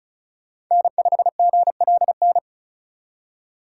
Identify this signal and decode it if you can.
Morse code, heard as a keyed tone, about 34 wpm, 700 Hz. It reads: N5GLN
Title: USA Call Signs - 5 Characters - 4x Character Spacing - 35wpm